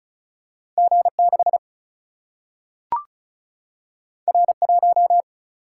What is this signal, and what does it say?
Morse code, heard as a keyed tone, about 36 wpm, 700 Hz. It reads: G 6 R 1
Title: Single-Letter Single-Number 35wpm